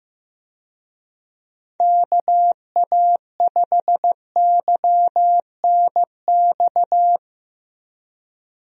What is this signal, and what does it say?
Morse code, heard as a keyed tone, about 15 wpm, 700 Hz. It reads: KA5YNX
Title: USA Call Signs - 6 Characters - Repeated 2x - 15wpm